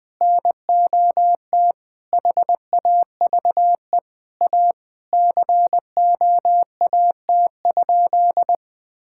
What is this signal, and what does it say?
Morse code, heard as a keyed tone, about 21 wpm, 700 Hz. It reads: NOT HAVE A COAT?
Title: Sentences from Top 1600 Words 20wpm